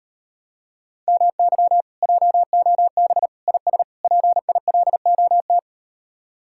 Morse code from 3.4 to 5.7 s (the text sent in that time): IS PILOT